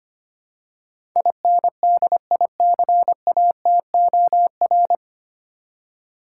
Morse code, heard as a keyed tone, about 25 wpm, 700 Hz. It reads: INDICATOR
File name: practice / 25wpm / 9 Letter Words - First 3 Letters Given - 25wpm